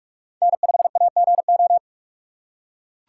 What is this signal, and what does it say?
Morse code, heard as a keyed tone, about 42 wpm, 700 Hz. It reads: N5AGO